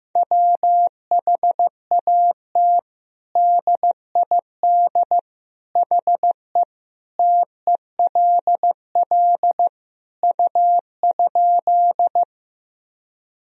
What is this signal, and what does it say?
Morse code, heard as a keyed tone, about 15 wpm, 700 Hz. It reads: WHAT DID HE TELL U?
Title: Sentences from Top 200 Words and Abbreviations 15wpm